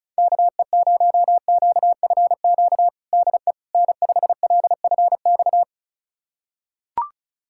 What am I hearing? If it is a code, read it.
Morse code, heard as a keyed tone, about 36 wpm, 690 Hz. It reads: KE0QFQ DE N5LFX
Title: USA Callsign DE Callsign 35wpm